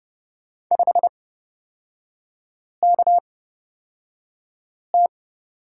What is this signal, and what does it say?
Morse code, heard as a keyed tone, about 29 wpm, spaced as for 5 wpm, 700 Hz. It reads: 5KT
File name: Lesson 36e - BT - Three Characters 30wpm